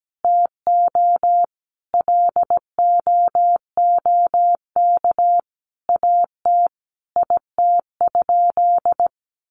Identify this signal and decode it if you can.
Morse code, heard as a keyed tone, about 17 wpm, 700 Hz. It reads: TO LOOK AT IT?
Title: Sentences from Top 100 Words 17wpm